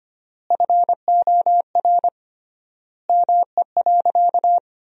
Morse code, heard as a keyed tone, about 25 wpm, 700 Hz. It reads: FOR ME.
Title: 3x Word-Spacing - Sentences from Top 300 Words - 25wpm